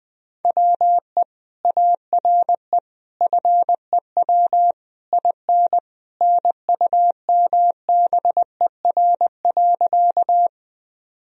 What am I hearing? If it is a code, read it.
Morse code, heard as a keyed tone, about 20 wpm, 700 Hz. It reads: WE ARE FEW IN NUMBER.
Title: Sentences from Top 1000 Words - Review Part 1 - 20wpm